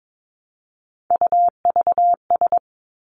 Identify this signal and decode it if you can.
Morse code, heard as a keyed tone, about 22 wpm, 700 Hz. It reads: UVS
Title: Three Challenging Letter-Number Combinations 22wpm